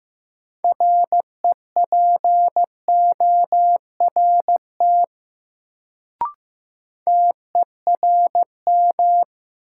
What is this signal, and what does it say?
Morse code, heard as a keyed tone, about 15 wpm, 700 Hz. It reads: REPORT TERM
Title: Sentences from Top 900 Words - New Word Drill - 15wpm